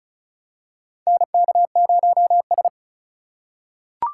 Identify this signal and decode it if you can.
Morse code, heard as a keyed tone, about 36 wpm, 700 Hz. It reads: NK0S
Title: USA Call Signs - 4 Characters - 6x Character Spacing - 35wpm